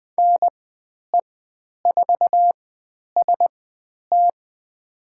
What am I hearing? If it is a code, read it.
Morse code, heard as a keyed tone, about 20 wpm, spaced as for 10 wpm, 700 Hz. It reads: NE4ST
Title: 109 Lesson 18g - Letter m - Call Signs 20wpm